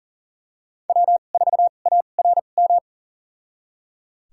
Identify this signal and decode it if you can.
Morse code, heard as a keyed tone, about 37 wpm, spaced as for 28 wpm, 700 Hz. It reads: W4ARM
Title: USA Call Signs - 5 Characters - 2x Character Spacing - 40wpm